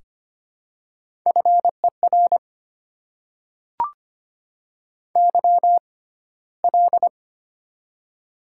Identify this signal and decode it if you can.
Morse code, heard as a keyed tone, about 25 wpm, spaced as for 15 wpm, 690 Hz. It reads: FER YL
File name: Top 90 QSO Elements - 6x Character Spacing - 25wpm